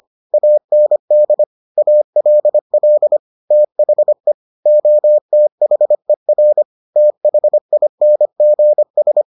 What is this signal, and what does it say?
Morse code, heard as a keyed tone, about 25 wpm, 600 Hz. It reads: AND ALL THE OTHER THINGS